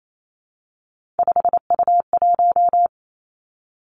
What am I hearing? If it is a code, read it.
Morse code, heard as a keyed tone, about 28 wpm, 700 Hz. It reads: 5U1
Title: Three Challenging Letter-Number Combinations 28wpm